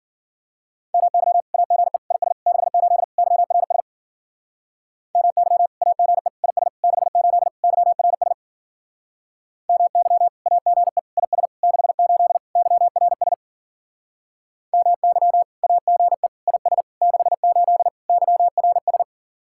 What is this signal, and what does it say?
Morse code, heard as a keyed tone, about 49 wpm, 700 Hz. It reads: MY AGE IS 68 YRS MY AGE IS 68 YRS MY AGE IS 68 YRS MY AGE IS 68 YRS
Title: Speed-Racing Common QSO Phrases Encore - 40wpm